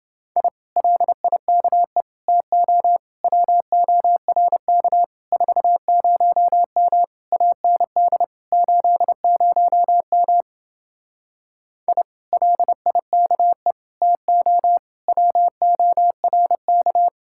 Morse code, heard as a keyed tone, about 29 wpm, 700 Hz. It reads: I LIKE TO WORK 40M AND 80M I LIKE TO WORK